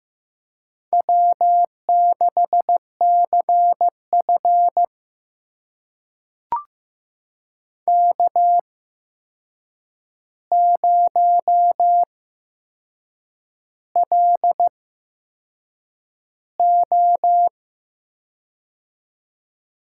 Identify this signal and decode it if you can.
Morse code, heard as a keyed tone, about 15 wpm, spaced as for 6 wpm, 700 Hz. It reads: W6CF K0LO
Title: USA Call Signs - 4 Characters - 8x Character Spacing - 15wpm